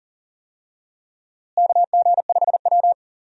Morse code, heard as a keyed tone, about 40 wpm, 700 Hz. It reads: KG5W
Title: USA Call Signs - 4 Characters - Repeated 2x - 40wpm